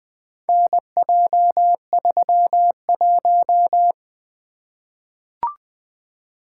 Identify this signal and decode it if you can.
Morse code, heard as a keyed tone, about 20 wpm, 700 Hz. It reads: NJ31
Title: Four Letter-Number Combinations - 20wpm